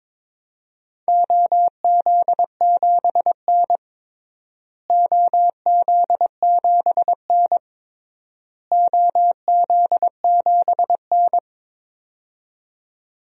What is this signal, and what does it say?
Morse code, heard as a keyed tone, about 22 wpm, 700 Hz. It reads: OZ7N OZ7N OZ7N
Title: DXCC-100 Call Signs - 4 Characters - Repeated 3x - 22wpm